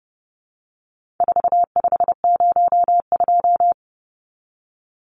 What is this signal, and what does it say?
Morse code, heard as a keyed tone, about 30 wpm, 700 Hz. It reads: 4502
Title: Numbers - 4-Digit 30wpm